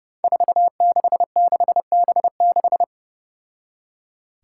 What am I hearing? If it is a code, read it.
Morse code, heard as a keyed tone, about 30 wpm, 700 Hz. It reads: 466B6
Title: Five Challenging Letter-Number Combinations 30wpm